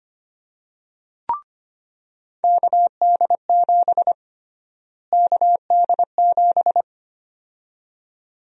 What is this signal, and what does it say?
Morse code, heard as a keyed tone, about 25 wpm, 700 Hz. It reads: KD7 KD7
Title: USA Call Sign Prefixes - Repeated 2x - 25wpm